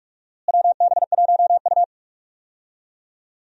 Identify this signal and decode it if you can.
Morse code, heard as a keyed tone, about 43 wpm, 700 Hz. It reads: WB1U
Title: USA Call Signs - 4 Characters - Repeated 3x - 45wpm